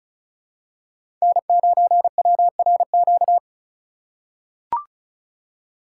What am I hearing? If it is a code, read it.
Morse code, heard as a keyed tone, about 36 wpm, 700 Hz. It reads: N9WRQ
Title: USA Call Signs - 5 Characters - 3x Character Spacing - 35wpm